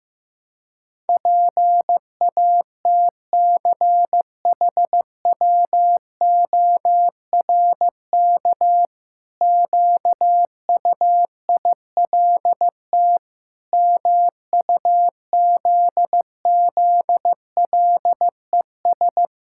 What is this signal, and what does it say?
Morse code, heard as a keyed tone, about 16 wpm, 700 Hz. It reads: PATCHWORK QUILT MUZZLES
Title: Pangram Challenge v2 - 15wpm